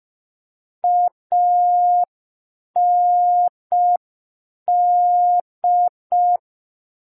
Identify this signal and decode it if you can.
Morse code, heard as a keyed tone, about 5 wpm, 710 Hz. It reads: AND